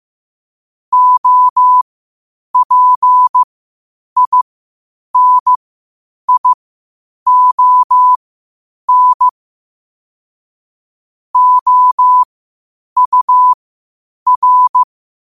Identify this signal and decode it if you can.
Morse code, heard as a keyed tone, about 15 wpm, spaced as for 8 wpm, 1000 Hz. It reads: OPINION OUR
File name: qotdfiles / 8wpm